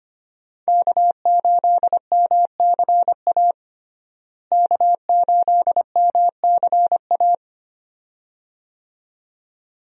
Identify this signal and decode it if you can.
Morse code, heard as a keyed tone, about 25 wpm, 700 Hz. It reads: K8MCA K8MCA
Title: USA Call Signs - 5 Characters - Repeated 2x - 25wpm